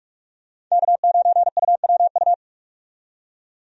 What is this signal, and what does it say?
Morse code, heard as a keyed tone, about 46 wpm, 700 Hz. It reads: K9UWU
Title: USA Call Signs - 5 Characters - 45wpm